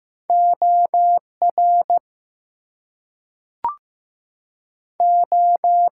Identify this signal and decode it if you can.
Morse code, heard as a keyed tone, about 15 wpm, 700 Hz. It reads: OR O
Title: Two Letter Words - 15wpm